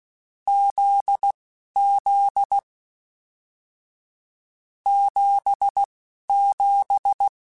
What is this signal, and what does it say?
Morse code, heard as a keyed tone, about 16 wpm, spaced as for 11 wpm, 800 Hz. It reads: ZZ 77
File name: Short Runs / Short Runs-mp3s / GZ GZ7 @10